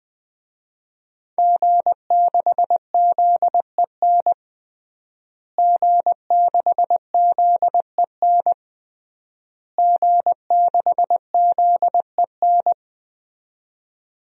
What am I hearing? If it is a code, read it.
Morse code, heard as a keyed tone, about 21 wpm, 700 Hz. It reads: G6ZEN G6ZEN G6ZEN
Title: DXCC-100 Call Signs - 5 Characters - Repeated 3x - 20wpm